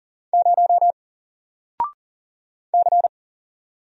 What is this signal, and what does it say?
Morse code, heard as a keyed tone, about 40 wpm, 700 Hz. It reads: 0 C
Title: Single Letter-Number Alt 40wpm